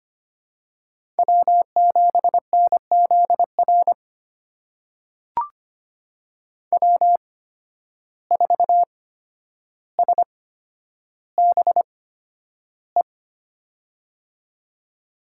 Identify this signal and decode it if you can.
Morse code, heard as a keyed tone, about 25 wpm, spaced as for 11 wpm, 700 Hz. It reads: W7NZR W4SBE
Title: USA Call Signs - 5 Characters - 8x Character Spacing - 25wpm